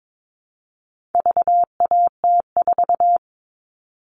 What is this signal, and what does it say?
Morse code, heard as a keyed tone, about 22 wpm, 700 Hz. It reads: VAT4
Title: All Cut Numbers - 4-Digit 22wpm